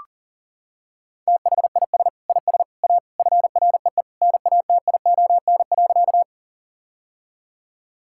Morse code, heard as a keyed tone, about 37 wpm, 700 Hz. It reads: THIS IS A FREE NATION.
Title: Sentences from Top 1000 Words - Review Part 6 - 40wpm